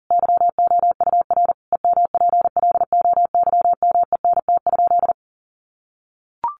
Morse code, heard as a keyed tone, about 40 wpm, 700 Hz. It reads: YOUR EMPLOYMENT?
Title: Sentences from Top 1500 Words 40wpm